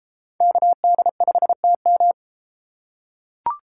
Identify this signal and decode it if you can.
Morse code, heard as a keyed tone, about 31 wpm, 700 Hz. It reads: KD5TM